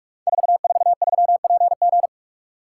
Morse code, heard as a keyed tone, about 44 wpm, 700 Hz. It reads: 443PG
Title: Code Groups - 5 Letter:Numbers - 45wpm